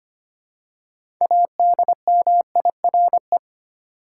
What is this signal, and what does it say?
Morse code, heard as a keyed tone, about 25 wpm, 700 Hz. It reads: ADMIRE
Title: Six Letter Words - 25wpm